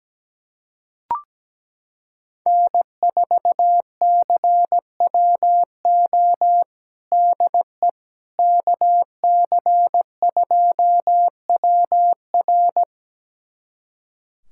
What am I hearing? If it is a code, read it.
Morse code, heard as a keyed tone, about 17 wpm, 700 Hz. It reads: N4CWO DE KC2WR
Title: USA Callsign DE Callsign - 5 Characters - 17wpm